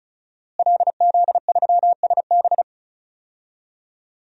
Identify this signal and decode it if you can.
Morse code, heard as a keyed tone, about 36 wpm, 700 Hz. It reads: LZ3SB
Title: DXCC-100 Call Signs - 5 Characters - Repeated 3x - 35wpm